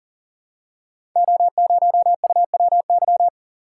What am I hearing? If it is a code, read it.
Morse code, heard as a keyed tone, about 40 wpm, 690 Hz. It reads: O0UWY
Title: Code Groups - 5 Letter:Numbers - 40wpm